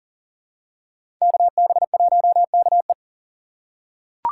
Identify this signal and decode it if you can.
Morse code, heard as a keyed tone, about 40 wpm, 700 Hz. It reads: KB1KE
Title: USA Call Signs - 5 Characters - 2x Character Spacing - 40wpm